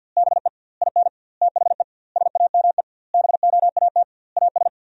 Morse code, heard as a keyed tone, about 52 wpm, 700 Hz. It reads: BE IN THE SAME BOAT AS
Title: 2x Word-Spacing - Sentences from Top 500 Words - 50wpm